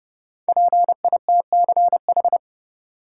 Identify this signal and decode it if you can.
Morse code, heard as a keyed tone, about 30 wpm, 700 Hz. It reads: PITCH
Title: Five Letter Words - 30wpm